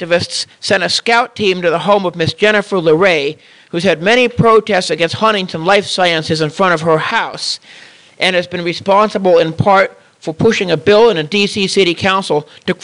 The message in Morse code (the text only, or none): none